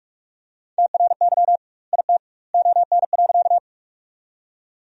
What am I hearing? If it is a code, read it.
Morse code, heard as a keyed tone, about 46 wpm, 700 Hz. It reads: TRY IT ON.
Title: 2x Word-Spacing - Sentences from Top 200 Words - 45wpm